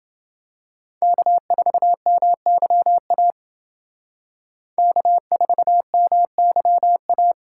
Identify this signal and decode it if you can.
Morse code, heard as a keyed tone, about 28 wpm, 700 Hz. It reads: K4MYA K4MYA